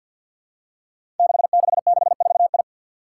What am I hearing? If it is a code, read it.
Morse code, heard as a keyed tone, about 50 wpm, 700 Hz. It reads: 6664I
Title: Five Challenging Letter-Number Combinations 50wpm